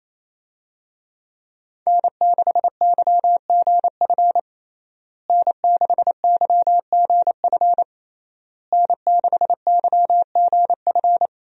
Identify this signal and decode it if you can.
Morse code, heard as a keyed tone, about 28 wpm, 700 Hz. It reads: N6YGF N6YGF N6YGF
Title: USA Call Signs - 5 Characters - Repeated 3x - 28wpm